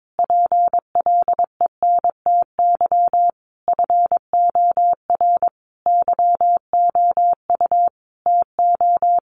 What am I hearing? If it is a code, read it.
Morse code, heard as a keyed tone, about 22 wpm, 700 Hz. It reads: PLENTY FOR YOU TO
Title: Sentences from Top 1300 Words 22wpm